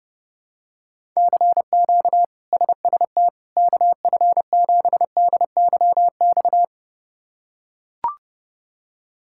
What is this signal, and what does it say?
Morse code, heard as a keyed tone, about 32 wpm, 700 Hz. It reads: CQ SST KF7DYX
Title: K1USN SST QSO Practice - 30wpm